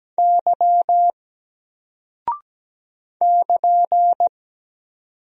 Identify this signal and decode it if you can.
Morse code, heard as a keyed tone, about 17 wpm, 700 Hz. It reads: Y <KN>
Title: Single Character (Letter:Number:Punctuation:Prosign) 17wpm